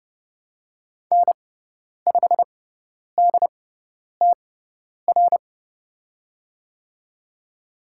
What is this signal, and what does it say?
Morse code, heard as a keyed tone, about 30 wpm, spaced as for 10 wpm, 700 Hz. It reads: N5DTR
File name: Lesson 17g - Letter u - Call Signs 30wpm